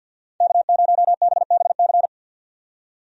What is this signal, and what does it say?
Morse code, heard as a keyed tone, about 53 wpm, 700 Hz. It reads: K0BBC
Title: USA Call Signs - 5 Characters - 50wpm